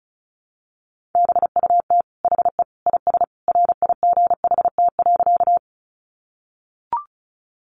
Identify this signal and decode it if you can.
Morse code, heard as a keyed tone, about 34 wpm, 700 Hz. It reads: BUT HE IS RIGHT.
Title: Sentences from Top 200 Words 35wpm